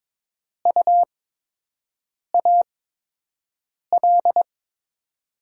Morse code, heard as a keyed tone, about 22 wpm, spaced as for 6 wpm, 700 Hz. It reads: UAL
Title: USA Call Signs - 5 Characters - 8x Character Spacing - 22wpm